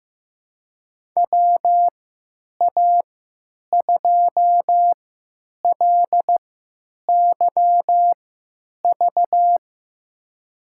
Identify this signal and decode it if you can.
Morse code, heard as a keyed tone, about 15 wpm, spaced as for 9 wpm, 700 Hz. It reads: WA2LYV
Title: USA Call Signs - 6 Characters - 3x Character Spacing - 15wpm